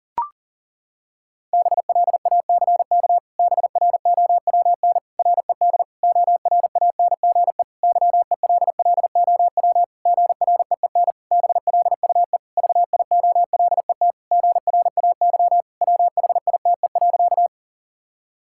Morse code, heard as a keyed tone, about 40 wpm, 700 Hz. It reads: BLACK BROWN RED ORANGE YELLOW GREEN BLUE VIOLET GRAY WHITE.
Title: Pangram Challenge v2 - 40wpm